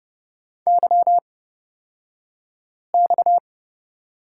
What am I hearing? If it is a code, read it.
Morse code, heard as a keyed tone, about 30 wpm, spaced as for 5 wpm, 700 Hz. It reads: YX
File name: Lesson 37d - Letter x - Two Characters 30wpm